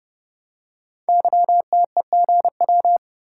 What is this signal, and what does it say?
Morse code, heard as a keyed tone, about 29 wpm, 700 Hz. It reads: YTEGW
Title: Code Groups - 5 Letter:Numbers - 30wpm